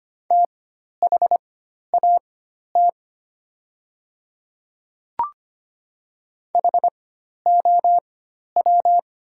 Morse code, heard as a keyed tone, about 25 wpm, spaced as for 12 wpm, 700 Hz. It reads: THAT HOW
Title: 4x Character Spacing - Top 100 Words - 25wpm